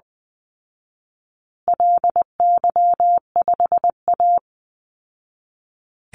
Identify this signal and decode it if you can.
Morse code, heard as a keyed tone, about 20 wpm, 700 Hz. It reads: LY5A
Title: Call Signs Repeated 2x - 20wpm